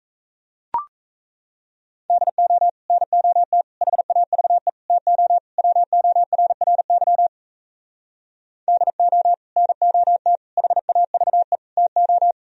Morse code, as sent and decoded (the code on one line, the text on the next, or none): -.. --- / -. --- - / .... .- ...- . / - --- / .-- --- .-. .-. -.-- / -.. --- / -. --- - / .... .- ...- . / - ---
DO NOT HAVE TO WORRY DO NOT HAVE TO